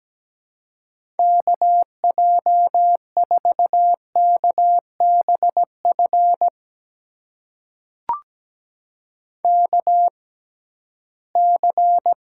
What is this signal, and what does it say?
Morse code, heard as a keyed tone, about 17 wpm, spaced as for 13 wpm, 700 Hz. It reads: KJ4KBF KC
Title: USA Call Signs - 6 Characters - 6x Character Spacing - 17wpm